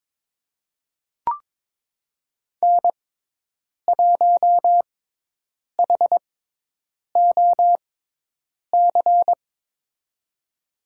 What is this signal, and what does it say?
Morse code, heard as a keyed tone, about 22 wpm, spaced as for 8 wpm, 700 Hz. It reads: N1HOC
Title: USA Call Signs - 5 Characters - 6x Character Spacing - 22wpm